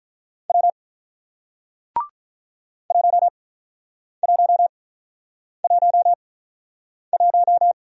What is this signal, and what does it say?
Morse code, heard as a keyed tone, about 47 wpm, spaced as for 9 wpm, 700 Hz. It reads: W 1111